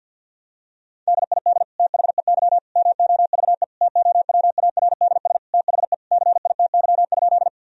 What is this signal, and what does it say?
Morse code, heard as a keyed tone, about 46 wpm, 700 Hz. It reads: DID THEY MOVE TOWARDS THE CITY?
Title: Sentences from Top 1000 Words - Review Part 2 - 50wpm